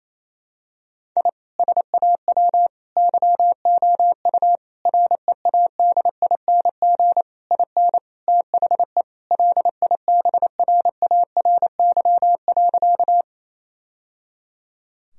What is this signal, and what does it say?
Morse code, heard as a keyed tone, about 28 wpm, 700 Hz. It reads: I SAW YOU READING IN THE LIBRARY.
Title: Sentences from Top 1000 Words - Review Part 4 - 28wpm